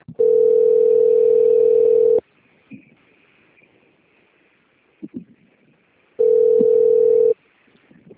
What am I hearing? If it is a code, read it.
Ringback tone (North America), 440 + 480 Hz.
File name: out- - - - - .9934